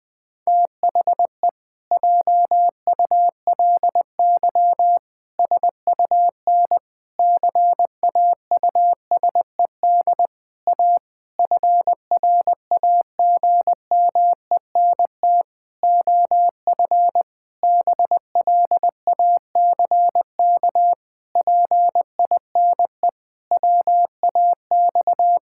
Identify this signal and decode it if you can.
Morse code, heard as a keyed tone, about 19 wpm, 700 Hz. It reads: THE JULY SUN CAUSED A FRAGMENT OF BLACK PINE WAX
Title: Pangram Challenge v2 - 20wpm